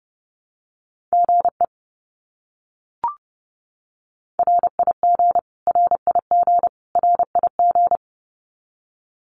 Morse code, heard as a keyed tone, about 30 wpm, 700 Hz. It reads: GE RIG RIG RIG
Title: Abbreviations Repeated 3x - 30wpm